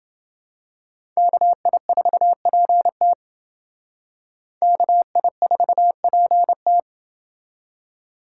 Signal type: Morse code

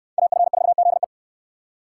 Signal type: Morse code

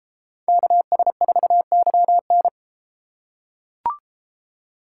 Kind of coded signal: Morse code

